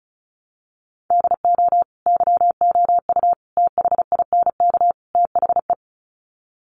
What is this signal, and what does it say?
Morse code, heard as a keyed tone, about 34 wpm, 700 Hz. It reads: DO YOU THINK THE
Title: Sets of 4 Words 35wpm